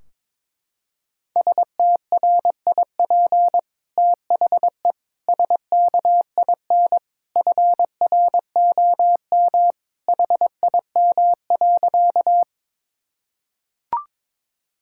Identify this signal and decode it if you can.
Morse code, heard as a keyed tone, about 22 wpm, 700 Hz. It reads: STRIP THE SKIN FROM HIM.
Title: Sentences from Top 1600 Words 22wpm